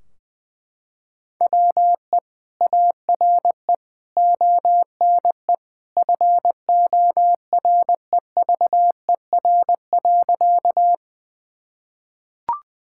Morse code, heard as a keyed tone, about 21 wpm, 700 Hz. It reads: WE ARE ONE FOREVER.
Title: Sentences from Top 1600 Words 20wpm